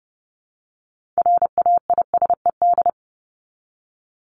Morse code, heard as a keyed tone, about 30 wpm, 700 Hz. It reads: RAISED
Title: ED Suffix - Top 100 Words with Base Words 30wpm